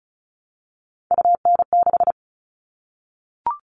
Morse code, heard as a keyed tone, about 34 wpm, 700 Hz. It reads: UN6
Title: All Cut Numbers - 3-Digit 35wpm